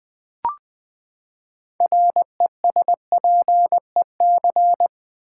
Morse code, heard as a keyed tone, about 20 wpm, 700 Hz. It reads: RESPEC
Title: Build A Word - 6 Letters At A Time - 12-Letter Words - 20wpm